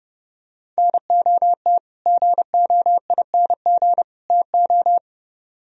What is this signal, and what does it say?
Morse code, heard as a keyed tone, about 30 wpm, 700 Hz. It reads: NOT GOING TO